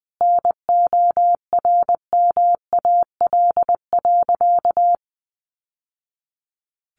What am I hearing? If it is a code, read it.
Morse code, heard as a keyed tone, about 20 wpm, 700 Hz. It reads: NORMAL.
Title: Sentences from Top 1200 Words 20wpm